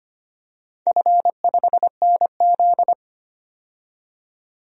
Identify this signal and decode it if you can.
Morse code, heard as a keyed tone, about 25 wpm, 700 Hz. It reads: F5NZ
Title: DXCC-100 Call Signs - 4 Characters - 25wpm